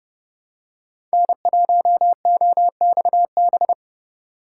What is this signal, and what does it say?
Morse code, heard as a keyed tone, about 30 wpm, 700 Hz. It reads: N1OXB
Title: USA Call Signs - 5 Characters - 4x Character Spacing - 30wpm